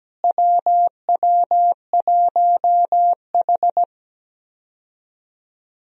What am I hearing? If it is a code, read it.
Morse code, heard as a keyed tone, about 17 wpm, 700 Hz. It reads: WW1H
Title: USA Call Signs - 4 Characters - Repeated 3x - 17wpm